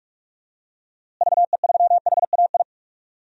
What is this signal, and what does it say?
Morse code, heard as a keyed tone, about 46 wpm, 700 Hz. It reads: VE3HAI
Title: DXCC-100 Call Signs - 6 Characters - Repeated 2x - 45wpm